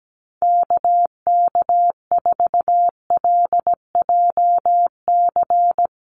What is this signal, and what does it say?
Morse code, heard as a keyed tone, about 17 wpm, 700 Hz. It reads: KK4LJC
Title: USA Call Signs Repeated 3x 17wpm